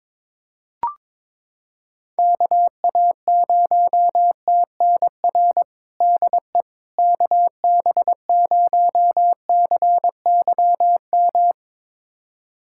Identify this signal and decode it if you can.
Morse code, heard as a keyed tone, about 22 wpm, 700 Hz. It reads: KA0TNR DE KB0CYM
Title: USA Callsign DE Callsign 22wpm